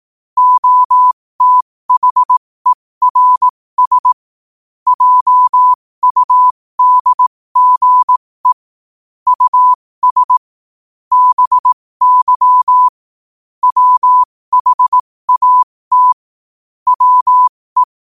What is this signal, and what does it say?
Morse code, heard as a keyed tone, about 18 wpm, spaced as for 15 wpm, 1000 Hz. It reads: OTHERS JUDGE US BY WHAT WE
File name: qotdfiles / 15wpm